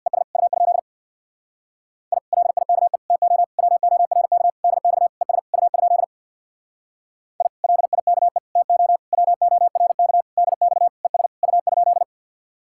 Morse code, heard as a keyed tone, about 59 wpm, 700 Hz. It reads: ES U? I LIKE TO WORK DX ES U? I LIKE TO WORK DX ES U?
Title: Speed-Racing Common QSO Phrases - 50wpm